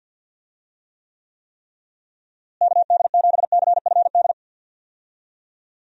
Morse code, heard as a keyed tone, about 52 wpm, 700 Hz. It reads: KD7CFD
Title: USA Call Signs - 6 Characters - Repeated 3x - 50wpm